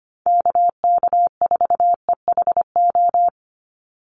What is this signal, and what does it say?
Morse code, heard as a keyed tone, about 25 wpm, 700 Hz. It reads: KK4EHO
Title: USA Call Signs 25wpm